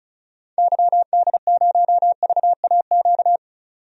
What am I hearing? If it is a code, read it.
Morse code, heard as a keyed tone, about 36 wpm, 700 Hz. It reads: YD0VAQ
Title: DXCC-100 Call Signs - 6 Characters - 35wpm